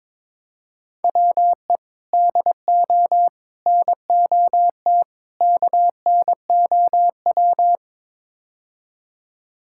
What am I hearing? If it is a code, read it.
Morse code, heard as a keyed tone, about 22 wpm, 700 Hz. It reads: WE DO NOT KNOW
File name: practice / 22wpm / Build A Phrase - 5-Word Phrases - 22wpm